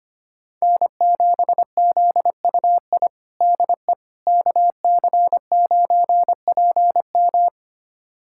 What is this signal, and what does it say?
Morse code, heard as a keyed tone, about 25 wpm, 700 Hz. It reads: N7ZUI DE KC9PM
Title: USA Callsign DE Callsign - 5 Characters - 25wpm